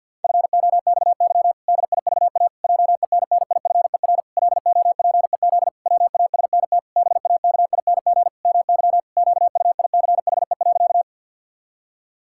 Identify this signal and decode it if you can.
Morse code, heard as a keyed tone, about 52 wpm, 700 Hz. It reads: FOXY DIVA JENNIFER LOPEZ WASNT BAKING MY QUICHE.
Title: Pangram Challenge v2 - 50wpm